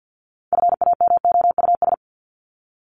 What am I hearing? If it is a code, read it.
Morse code, heard as a keyed tone, about 55 wpm, 700 Hz. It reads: FAMOUS
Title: Top 1251-1500 Words 50wpm